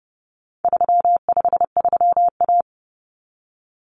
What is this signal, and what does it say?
Morse code, heard as a keyed tone, about 30 wpm, 700 Hz. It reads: 353A
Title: Common Cut Numbers - 4-Digit 30wpm